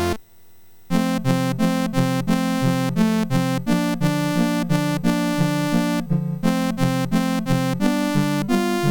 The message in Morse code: none